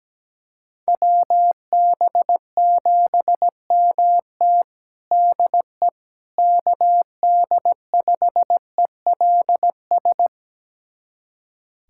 Morse code, heard as a keyed tone, about 17 wpm, 700 Hz. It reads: WB7MT DE KD5ELS
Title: USA Callsign DE Callsign 17wpm